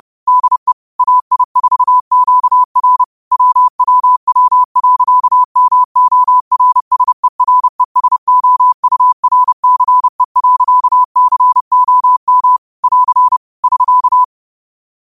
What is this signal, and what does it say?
Morse code, heard as a keyed tone, about 30 wpm, 1000 Hz. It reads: DE AI4QR WWW.MORSERESOURCE.COM <AR> <SK>
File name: qotdfiles / 30wpm